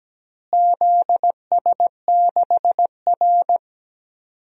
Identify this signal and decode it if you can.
Morse code, heard as a keyed tone, about 17 wpm, 700 Hz. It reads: ZS6R
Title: DXCC-100 Call Signs - 4 Characters - Repeated 2x - 17wpm